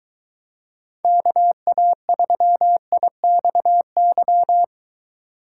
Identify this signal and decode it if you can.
Morse code, heard as a keyed tone, about 23 wpm, 700 Hz. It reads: KA3IXY